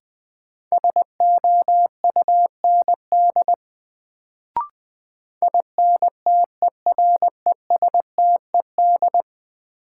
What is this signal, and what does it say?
Morse code, heard as a keyed tone, about 20 wpm, 700 Hz. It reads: SOUND INTERESTED
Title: Sentences from Top 900 Words - New Word Drill - 20wpm